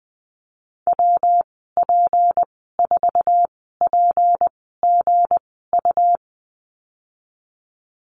Morse code, heard as a keyed tone, about 19 wpm, spaced as for 14 wpm, 700 Hz. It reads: WP4PGU
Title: 193 Lesson 31g - Letter v - Call Signs 20wpm